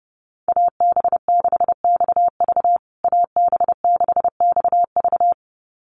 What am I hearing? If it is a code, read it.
Morse code, heard as a keyed tone, about 30 wpm, 700 Hz. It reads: AB6XV AB6XV
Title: USA Call Signs Repeated 2x 30wpm